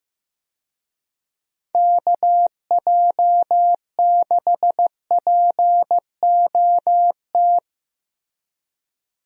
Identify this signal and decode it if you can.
Morse code, heard as a keyed tone, about 15 wpm, 700 Hz. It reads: KJ6POT
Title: USA Call Signs - 6 Characters - Repeated 2x - 15wpm